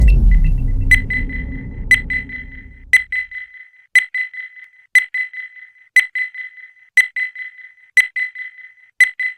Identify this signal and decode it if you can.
There is no code.